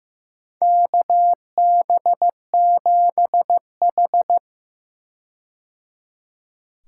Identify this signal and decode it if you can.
Morse code, heard as a keyed tone, about 15 wpm, 700 Hz. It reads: KB7H
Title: K1USN SST QSO Practice - 15wpm